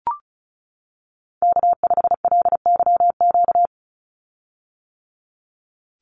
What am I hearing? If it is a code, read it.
Morse code, heard as a keyed tone, about 34 wpm, 700 Hz. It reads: K5LYQ
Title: USA Call Signs 35wpm